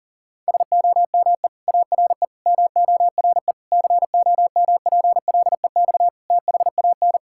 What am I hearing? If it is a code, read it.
Morse code, heard as a keyed tone, about 40 wpm, 700 Hz. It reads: SOME ARE MORE COMPLEX THAN
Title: Sentences from Top 1000 Words - Review Part 6 - 40wpm